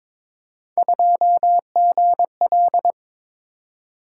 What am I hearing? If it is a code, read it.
Morse code, heard as a keyed tone, about 21 wpm, 700 Hz. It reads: 2GL
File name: practice / 22wpm / Three Letter-Number Combinations - 22wpm